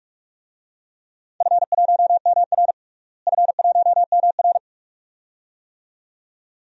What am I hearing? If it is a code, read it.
Morse code, heard as a keyed tone, about 44 wpm, 700 Hz. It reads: F1MR F1MR
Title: DXCC-100 Call Signs - 4 Characters - Repeated 2x - 45wpm